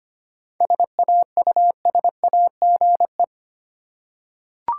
Morse code, heard as a keyed tone, about 25 wpm, 700 Hz. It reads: SAUSAGE
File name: practice / 25wpm / Difficult Words with Hints - 25wpm